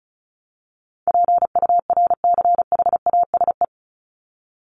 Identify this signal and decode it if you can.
Morse code, heard as a keyed tone, about 34 wpm, 700 Hz. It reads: PURCHASE
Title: Top 501-750 Words 35wpm